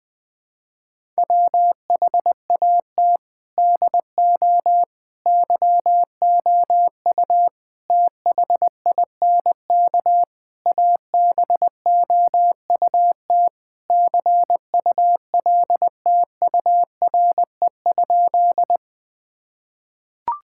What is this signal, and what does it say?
Morse code, heard as a keyed tone, about 20 wpm, 700 Hz. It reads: WHAT DO YOU THINK ABOUT CULTURE?
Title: Sentences from Top 1000 Words - Review Part 6 - 20wpm